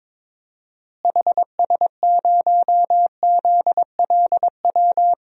Morse code, heard as a keyed tone, about 22 wpm, 700 Hz. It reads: HS0ZLW
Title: DXCC-100 Call Signs - 6 Characters - Repeated 2x - 22wpm